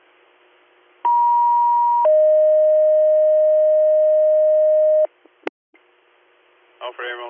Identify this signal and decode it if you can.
Two-tone page, heard as a one-second tone, then a three-second tone, about 950 then 620 Hz, with speech. It is heard from 1.0 to 5.1 s.